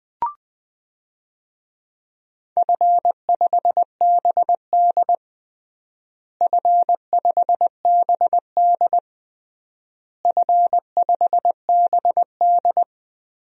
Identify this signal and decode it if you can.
Morse code, heard as a keyed tone, about 20 wpm, 700 Hz. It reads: F5BD F5BD F5BD
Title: DXCC-100 Call Signs - 4 Characters - Repeated 3x - 20wpm